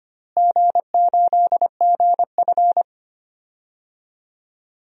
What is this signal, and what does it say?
Morse code, heard as a keyed tone, about 25 wpm, 700 Hz. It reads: G8GF
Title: DXCC-100 Call Signs - 4 Characters - Repeated 2x - 25wpm